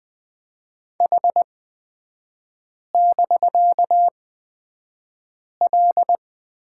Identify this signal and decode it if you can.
Morse code, heard as a keyed tone, about 20 wpm, spaced as for 5 wpm, 700 Hz. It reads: H<BK>L
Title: 255 Lesson 41e - Period - Three Characters 20wpm